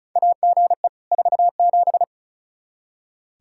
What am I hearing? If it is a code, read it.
Morse code, heard as a keyed tone, about 36 wpm, 700 Hz. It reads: AGE 47
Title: K1USN SST QSO Practice 35wpm